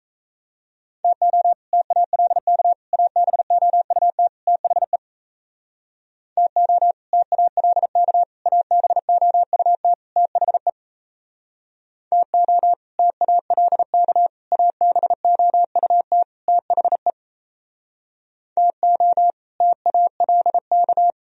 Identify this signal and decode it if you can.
Morse code, heard as a keyed tone, about 36 wpm, 700 Hz. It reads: TO TALK ABOUT THE TO TALK ABOUT THE TO TALK ABOUT THE TO TALK